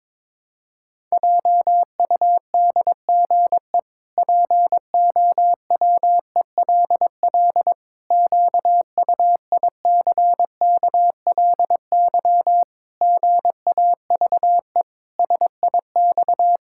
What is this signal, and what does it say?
Morse code, heard as a keyed tone, about 22 wpm, 700 Hz. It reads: JUDGE POWELL QUICKLY GAVE SIX
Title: Pangram Challenge v2 - 22wpm